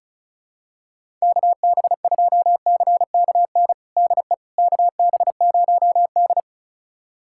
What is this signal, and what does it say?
Morse code, heard as a keyed tone, about 36 wpm, 700 Hz. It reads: KB2CKN DE KB0D
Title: USA Callsign DE Callsign 35wpm